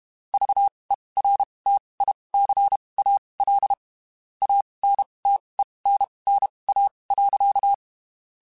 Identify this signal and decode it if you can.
Morse code, heard as a keyed tone, about 32 wpm, spaced as for 23 wpm, 800 Hz. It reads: VERTICAL ANTENNA.
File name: (spaced) / QSOs Spaced-mp3s / kc7haa @30 sp